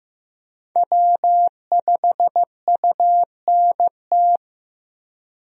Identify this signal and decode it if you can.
Morse code, heard as a keyed tone, about 16 wpm, 700 Hz. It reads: W5UNT
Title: USA Call Signs - 5 Characters - Repeated 3x - 15wpm